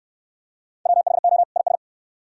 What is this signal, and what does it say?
Morse code, heard as a keyed tone, about 66 wpm, 700 Hz. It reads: WHO IS